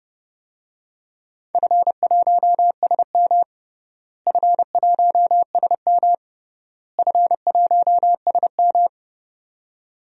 Morse code, heard as a keyed tone, about 32 wpm, 700 Hz. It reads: F1SM F1SM F1SM
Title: DXCC-100 Call Signs - 4 Characters - Repeated 3x - 30wpm